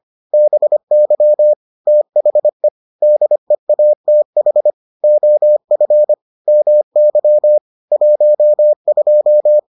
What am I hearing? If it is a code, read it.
Morse code, heard as a keyed tone, about 25 wpm, 600 Hz. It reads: BY THE DEATH OF MY 12